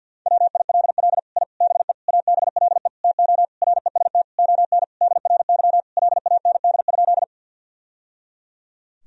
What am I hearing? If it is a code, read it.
Morse code, heard as a keyed tone, about 50 wpm, 700 Hz. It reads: WILL I BE ABLE TO REST ON DRY LAND?
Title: Sentences from Top 1000 Words - Review Part 3 - 50wpm